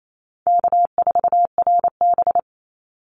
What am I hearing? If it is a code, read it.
Morse code, heard as a keyed tone, about 28 wpm, 700 Hz. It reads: K4RB
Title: Easy Call Signs - 28wpm